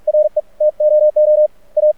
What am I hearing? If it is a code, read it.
Morse code, heard as a keyed tone, about 46 wpm, 610 Hz. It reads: RE TOO A
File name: archived / CRSnet